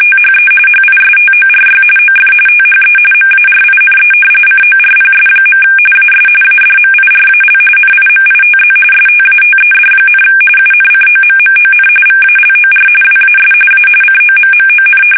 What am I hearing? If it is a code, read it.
RTTY, about 45.45 baud. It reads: THE HISTORY OF THE MARINE CORPS. AS ADMIRAL NIMITZ STATED, UNCOMMON VALOR WAS A COMMON
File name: fsk850 / battleshipnj